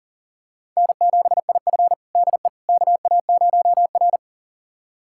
Morse code, heard as a keyed tone, about 40 wpm, 700 Hz. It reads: N7IF DE KA0R
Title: USA Callsign DE Callsign - 4 Characters - 40wpm